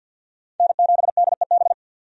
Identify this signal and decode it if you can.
Morse code, heard as a keyed tone, about 51 wpm, 700 Hz. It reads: N7DEB
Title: USA Call Signs - 5 Characters - 50wpm